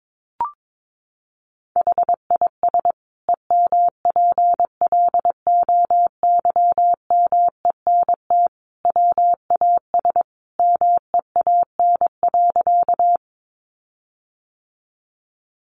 Morse code, heard as a keyed tone, about 22 wpm, 700 Hz. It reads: HIS EMPLOYMENT WAS MEAN.
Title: Sentences from Top 1500 Words 22wpm